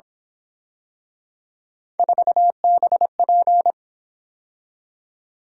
Morse code, heard as a keyed tone, about 26 wpm, 700 Hz. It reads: 4BP
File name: Speed-Racing - Three Letter-Number Combinations - 22wpm